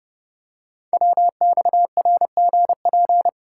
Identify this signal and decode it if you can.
Morse code, heard as a keyed tone, about 30 wpm, 700 Hz. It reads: WXRGP
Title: Code Groups - 5 Letters - 30wpm